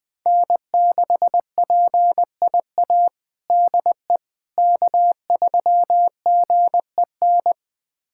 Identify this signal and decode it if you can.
Morse code, heard as a keyed tone, about 21 wpm, 700 Hz. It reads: N6PIA DE K3GEN
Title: USA Callsign DE Callsign - 5 Characters - 20wpm